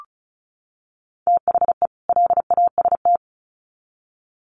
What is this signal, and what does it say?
Morse code, heard as a keyed tone, about 34 wpm, 700 Hz. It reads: THE LAST
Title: Sets of 2 Words 35wpm